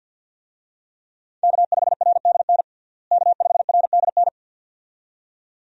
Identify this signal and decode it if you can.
Morse code, heard as a keyed tone, about 50 wpm, 700 Hz. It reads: K5RDN K5RDN
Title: USA Call Signs - 5 Characters - Repeated 2x - 50wpm